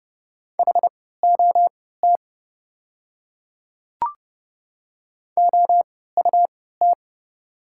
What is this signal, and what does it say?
Morse code, heard as a keyed tone, about 30 wpm, spaced as for 17 wpm, 700 Hz. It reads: HOT OUT
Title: 3x Character Spacing - Top 100 Words - 30wpm